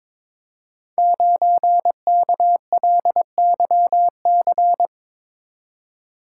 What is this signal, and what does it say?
Morse code, heard as a keyed tone, about 22 wpm, 700 Hz. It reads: 9KLYC
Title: Code Groups - 5 Letter:Numbers - 22wpm